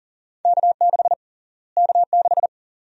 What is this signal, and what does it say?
Morse code, heard as a keyed tone, about 40 wpm, 700 Hz. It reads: K6 K6
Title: USA Call Sign Prefixes - Repeated 2x - 40wpm